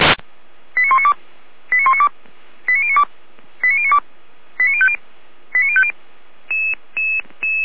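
Five-tone selective call (ZVEI), heard as quick runs of five tones, tones of about 80 ms. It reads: 89192 89192 89092 89092 89050 89050
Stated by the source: ZVEI five-tone selective call received on SDR, Feuerwehr Orlamünde (fire brigade).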